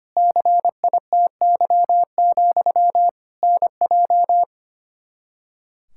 Morse code, heard as a keyed tone, about 25 wpm, 700 Hz. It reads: CITY, NJ
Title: Practice QSO - Exchange 1 - Beginning - 25wpm